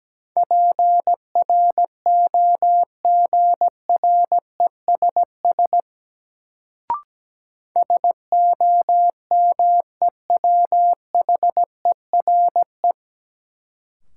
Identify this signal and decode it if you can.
Morse code, heard as a keyed tone, about 17 wpm, 700 Hz. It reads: PROGRESS SOMEWHERE
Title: Sentences from Top 1400 Words - New Word Drill - 17wpm